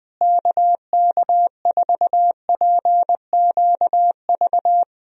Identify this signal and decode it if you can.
Morse code, heard as a keyed tone, about 20 wpm, 700 Hz. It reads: KK4PQV
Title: USA Call Signs - 6 Characters - Repeated 2x - 20wpm